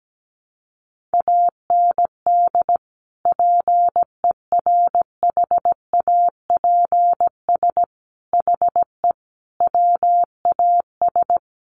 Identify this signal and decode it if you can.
Morse code, heard as a keyed tone, about 17 wpm, 700 Hz. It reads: AND PERHAPS HE WAS
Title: Sentences from Top 900 Words 17wpm